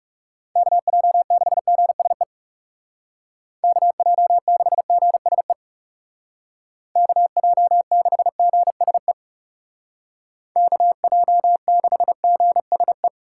Morse code, heard as a keyed tone, about 39 wpm, 700 Hz. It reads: KJ6GSE KJ6GSE KJ6GSE KJ6GSE